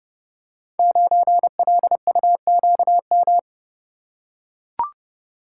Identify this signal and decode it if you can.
Morse code, heard as a keyed tone, about 30 wpm, 700 Hz. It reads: 9LUQM
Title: Code Groups - 5 Letter:Numbers - 30wpm